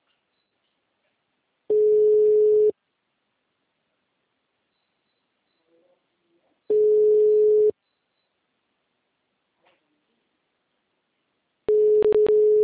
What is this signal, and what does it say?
Ringback tone (Europe, 425 Hz).